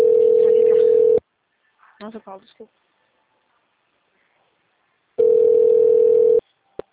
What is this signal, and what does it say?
Ringback tone (North America), 440 + 480 Hz, with speech.
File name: out- -2016- - - .57540